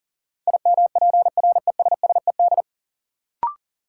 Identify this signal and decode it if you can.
Morse code, heard as a keyed tone, about 40 wpm, 700 Hz. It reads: IMPRESSED
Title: Sentences from Top 1600 Words - New Word Drill - 40wpm